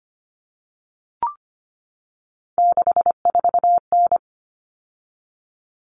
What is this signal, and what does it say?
Morse code, heard as a keyed tone, about 25 wpm, 700 Hz. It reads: 64N
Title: All Cut Numbers - 3-Digit 25wpm